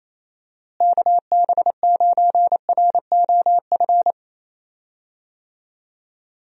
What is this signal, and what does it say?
Morse code, heard as a keyed tone, about 28 wpm, 700 Hz. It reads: KB9ROF
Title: USA Call Signs - 6 Characters - Repeated 3x - 28wpm